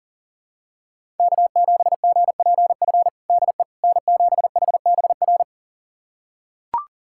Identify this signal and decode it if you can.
Morse code, heard as a keyed tone, about 37 wpm, 700 Hz. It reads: K7GPF DE N7HBR
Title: USA Callsign DE Callsign - 5 Characters - 40wpm